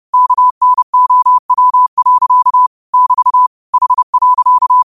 Morse code, heard as a keyed tone, about 30 wpm, 1000 Hz. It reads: KNOW. <BT> H.
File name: qotdfiles / 30wpm